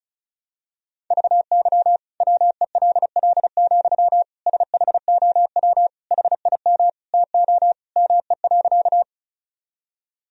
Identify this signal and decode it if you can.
Morse code, heard as a keyed tone, about 35 wpm, 700 Hz. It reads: VY WELL, SHOW HIM TO ME.
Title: Sentences from Top 200 Words and Abbreviations 35wpm